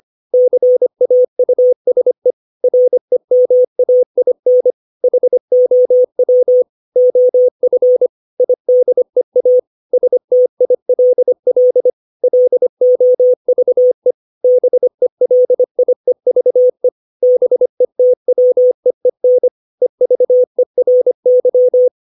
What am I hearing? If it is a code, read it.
Morse code, heard as a keyed tone, about 25 wpm, 500 Hz. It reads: CAUSE REMAIN HOW OF IDEA STILL LOVE BELIEVE BETWEEN EVERY
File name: 632 words / 632 Words 25WPM